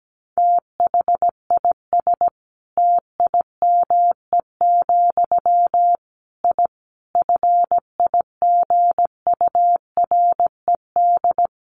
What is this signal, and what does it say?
Morse code, heard as a keyed tone, about 17 wpm, 700 Hz. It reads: THIS TIME, I FIGURED
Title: A Sound Of Thunder 17wpm